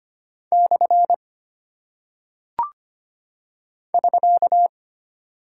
Morse code, heard as a keyed tone, about 25 wpm, 700 Hz. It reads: / <SK>
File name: Single Punctuation-Prosign - 25wpm